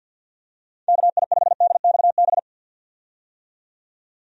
Morse code, heard as a keyed tone, about 51 wpm, 700 Hz. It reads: KI5DXB
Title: USA Call Signs - 6 Characters - 50wpm